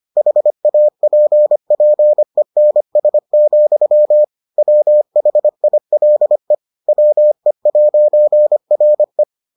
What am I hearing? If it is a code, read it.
Morse code, heard as a keyed tone, about 25 wpm, 600 Hz. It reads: HAPPENS, WHILE WE'RE